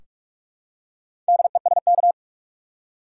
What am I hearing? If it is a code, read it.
Morse code, heard as a keyed tone, about 42 wpm, 700 Hz. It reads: DESK
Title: Four Letter Words - 45wpm